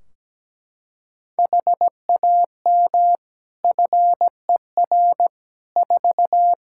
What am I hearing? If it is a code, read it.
Morse code, heard as a keyed tone, about 17 wpm, 700 Hz. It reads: HAM FER 4
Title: Practice QSO - Exchange 3 - More Info - 17wpm